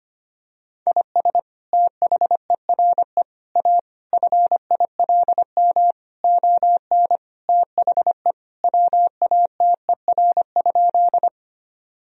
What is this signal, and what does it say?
Morse code, heard as a keyed tone, about 25 wpm, 700 Hz. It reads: IS THERE A FILM ON THE WATER?
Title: Sentences from Top 1000 Words - Review Part 3 - 25wpm